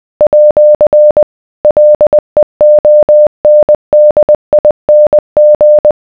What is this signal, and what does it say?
Morse code, heard as a keyed tone, about 20 wpm, 600 Hz. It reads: @ LEONDING